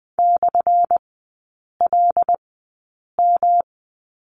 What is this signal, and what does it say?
Morse code, heard as a keyed tone, about 20 wpm, spaced as for 8 wpm, 700 Hz. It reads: /LM
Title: 178 Lesson 29e - Slash - Three Characters 20wpm